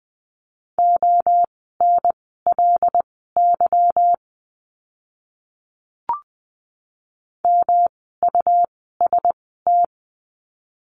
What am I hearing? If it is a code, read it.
Morse code, heard as a keyed tone, about 20 wpm, spaced as for 14 wpm, 700 Hz. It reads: ONLY MUST
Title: 237 Lesson 38f - Letter q - Words 20wpm